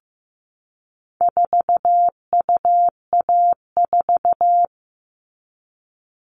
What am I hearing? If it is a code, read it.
Morse code, heard as a keyed tone, about 15 wpm, 700 Hz. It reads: 4UA4
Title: All Cut Numbers - 4-Digit 15wpm